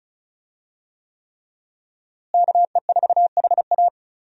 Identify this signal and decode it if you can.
Morse code, heard as a keyed tone, about 36 wpm, 700 Hz. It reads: KE4HA
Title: USA Call Signs - 5 Characters - Repeated 3x - 35wpm